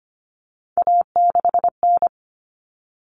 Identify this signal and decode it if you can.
Morse code, heard as a keyed tone, about 25 wpm, 700 Hz. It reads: A6N
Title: All Cut Numbers - 3-Digit 25wpm